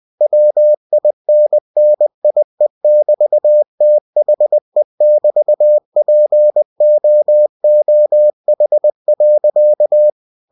Morse code, heard as a keyed tone, about 20 wpm, 600 Hz. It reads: WINNIE<BT>THE<BT>POOH.
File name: cw-books / winnie-the-pooh / winnie-20wpm-page-0006